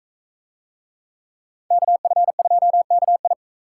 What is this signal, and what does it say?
Morse code, heard as a keyed tone, about 44 wpm, 700 Hz. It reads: KF2KI